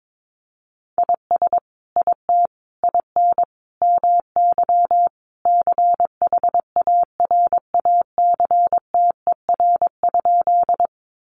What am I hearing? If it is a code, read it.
Morse code, heard as a keyed tone, about 22 wpm, 700 Hz. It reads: IS IT IN MY CHARACTER?
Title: Sentences from Top 1200 Words 22wpm